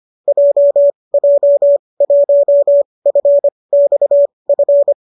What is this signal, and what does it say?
Morse code, heard as a keyed tone, about 25 wpm, spaced as for 20 wpm, 570 Hz. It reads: JJ1FXF
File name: qso208 20